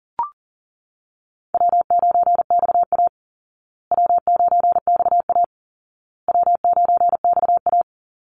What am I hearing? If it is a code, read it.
Morse code, heard as a keyed tone, about 40 wpm, 700 Hz. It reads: W9XA W9XA W9XA
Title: Call Signs Repeated 2x - 40wpm